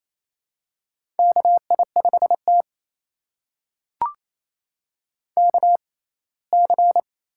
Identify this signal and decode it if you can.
Morse code, heard as a keyed tone, about 27 wpm, spaced as for 19 wpm, 700 Hz. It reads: KI5T KC
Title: USA Call Signs - 4 Characters - 6x Character Spacing - 28wpm